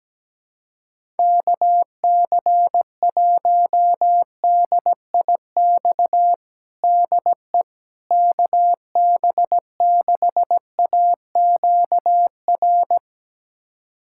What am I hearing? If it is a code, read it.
Morse code, heard as a keyed tone, about 17 wpm, 700 Hz. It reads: KC1DIX DE KB6AQR
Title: USA Callsign DE Callsign 17wpm